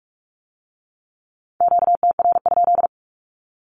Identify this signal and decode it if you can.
Morse code, heard as a keyed tone, about 46 wpm, 700 Hz. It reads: QTR?
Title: Q-Codes with Spoken Meaning - 45wpm